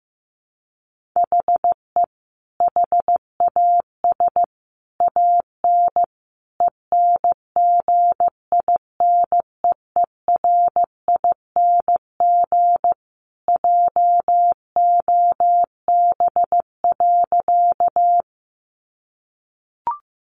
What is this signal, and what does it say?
Morse code, heard as a keyed tone, about 15 wpm, 700 Hz. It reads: HE HAS AN ENGINEERING JOB.
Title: Sentences from Top 1500 Words 15wpm